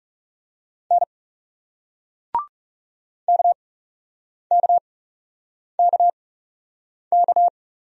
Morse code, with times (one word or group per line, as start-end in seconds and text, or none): 0.9-1.0 N
3.3-7.5 KKKK